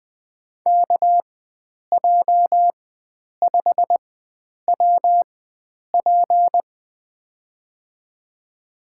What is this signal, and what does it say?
Morse code, heard as a keyed tone, about 20 wpm, spaced as for 9 wpm, 700 Hz. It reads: KJ5WP
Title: USA Call Signs - 5 Characters - 4x Character Spacing - 20wpm